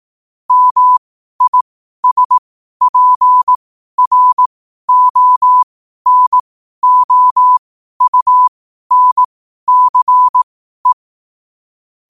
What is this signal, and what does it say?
Morse code, heard as a keyed tone, about 18 wpm, spaced as for 12 wpm, 1000 Hz. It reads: MISPRONOUNCE
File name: qotdfiles / 12wpm